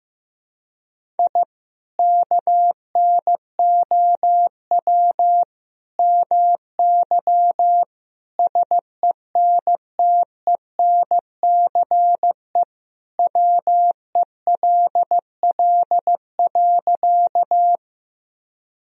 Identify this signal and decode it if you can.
Morse code, heard as a keyed tone, about 15 wpm, 700 Hz. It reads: I KNOW MY SENTENCE WELL.
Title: Sentences from Top 1000 Words - Review Part 1 - 15wpm